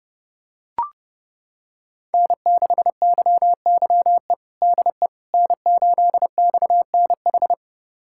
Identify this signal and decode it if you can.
Morse code, heard as a keyed tone, about 30 wpm, 700 Hz. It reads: N6YYE DE N8XNH
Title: USA Callsign DE Callsign - 5 Characters - 30wpm